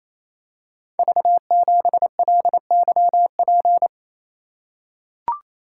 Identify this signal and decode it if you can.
Morse code, heard as a keyed tone, about 29 wpm, 700 Hz. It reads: V7LYP
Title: Code Groups - 5 Letter:Numbers - 28wpm